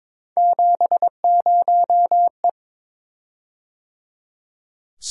Morse code, heard as a keyed tone, about 22 wpm, 700 Hz. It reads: 70E
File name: practice / 22wpm / Three Letter-Number Combinations - 22wpm